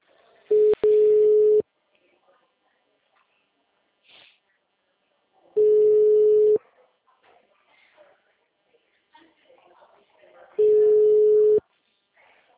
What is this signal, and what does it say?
Ringback tone (Europe, 425 Hz), with speech.